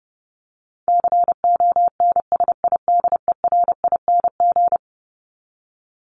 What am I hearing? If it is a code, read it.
Morse code, heard as a keyed tone, about 30 wpm, 700 Hz. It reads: CONSIDERING
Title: ING Suffix - Top 100 Words with Base Words 30wpm